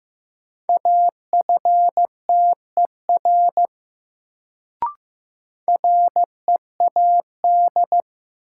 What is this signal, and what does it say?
Morse code, heard as a keyed tone, about 15 wpm, 700 Hz. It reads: AFTER READ
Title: Sentences from Top 200 Words - New Word Drill - 15wpm